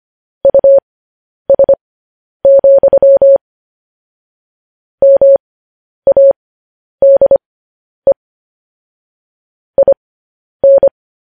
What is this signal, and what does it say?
Morse code, heard as a keyed tone, about 25 wpm, spaced as for 10 wpm, 550 Hz. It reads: US, MADE IN
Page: news headlines in morse code at 10 wpm